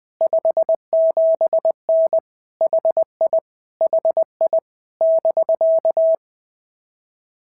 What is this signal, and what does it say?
Morse code, heard as a keyed tone, about 20 wpm, 650 Hz. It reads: 57N HI HI <BK>
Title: POTA QSO Practice - 20wpm